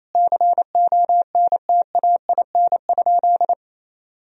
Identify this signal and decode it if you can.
Morse code, heard as a keyed tone, about 28 wpm, 700 Hz. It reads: CONTAIN?
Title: Sentences from Top 1000 Words - Review Part 2 - 28wpm